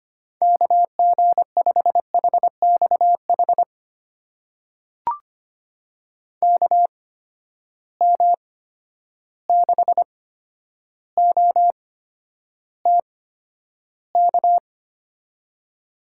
Morse code, heard as a keyed tone, about 25 wpm, spaced as for 11 wpm, 700 Hz. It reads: KG5HXH KM6OTK
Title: USA Call Signs - 6 Characters - 8x Character Spacing - 25wpm